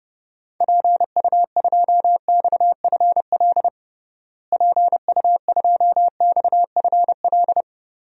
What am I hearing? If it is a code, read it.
Morse code, heard as a keyed tone, about 30 wpm, 700 Hz. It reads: PU2XFL PU2XFL
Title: DXCC-100 Call Signs - 6 Characters - Repeated 2x - 30wpm